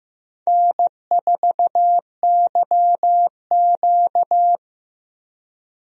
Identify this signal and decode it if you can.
Morse code, heard as a keyed tone, about 15 wpm, 700 Hz. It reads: N4YQ
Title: USA Call Signs - 4 Characters - Repeated 2x - 15wpm